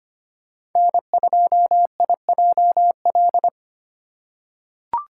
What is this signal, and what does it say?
Morse code, heard as a keyed tone, about 25 wpm, 700 Hz. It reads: N2IJL
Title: USA Call Signs - 5 Characters - 2x Character Spacing - 25wpm